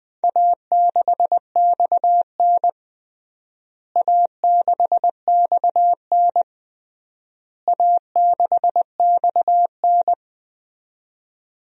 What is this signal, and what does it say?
Morse code, heard as a keyed tone, about 20 wpm, 700 Hz. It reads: A6XN A6XN A6XN
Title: DXCC-100 Call Signs - 4 Characters - Repeated 3x - 20wpm